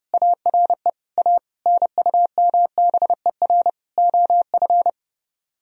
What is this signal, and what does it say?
Morse code, heard as a keyed tone, about 30 wpm, 700 Hz. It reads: ARE A NUMBER OF